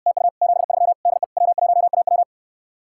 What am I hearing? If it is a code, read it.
Morse code, heard as a keyed tone, about 69 wpm, 700 Hz. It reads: TU 73 DE W1AW